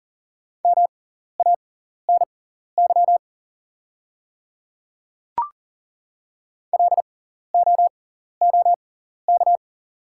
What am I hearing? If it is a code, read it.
Morse code, heard as a keyed tone, about 40 wpm, spaced as for 14 wpm, 690 Hz. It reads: MANY LOOK
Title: 6x Character Spacing - Top 100 Words - 40wpm